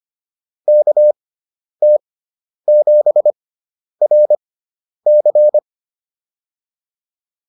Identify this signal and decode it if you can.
Morse code, heard as a keyed tone, about 25 wpm, spaced as for 10 wpm, 600 Hz. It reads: KT7RC